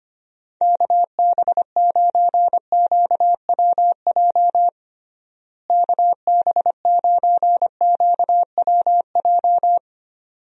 Morse code, heard as a keyed tone, about 25 wpm, 700 Hz. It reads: KB9QWJ KB9QWJ
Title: USA Call Signs - 6 Characters - Repeated 2x - 25wpm